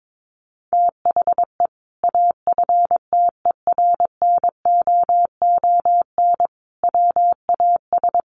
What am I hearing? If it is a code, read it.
Morse code, heard as a keyed tone, about 22 wpm, 700 Hz. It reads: THE AFTERNOON WAS
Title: Sentences from Top 1500 Words 22wpm